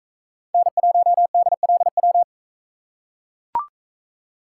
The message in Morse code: N1DLW

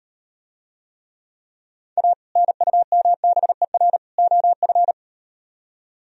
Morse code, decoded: A NUMBER OF

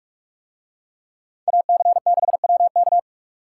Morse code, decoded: AC6WK

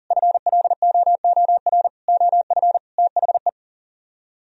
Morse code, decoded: FLOOR OF THE

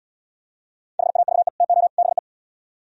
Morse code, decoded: HAVE TO BE

